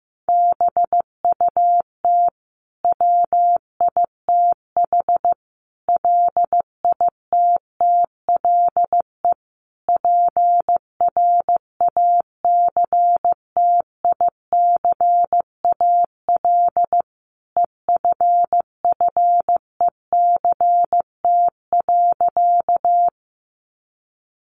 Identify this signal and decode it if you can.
Morse code, heard as a keyed tone, about 14 wpm, 700 Hz. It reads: BUT WITH LITTLE PRACTICAL EFFECT.
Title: Sentences from Top 1400 Words 15wpm